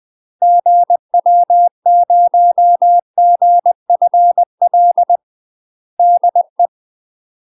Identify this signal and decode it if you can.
Morse code, heard as a keyed tone, about 20 wpm, 700 Hz. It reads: GW0GFL DE
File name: UK MASTER QSOs TRAINING / 20WPM / UK MASTER QSOs 0050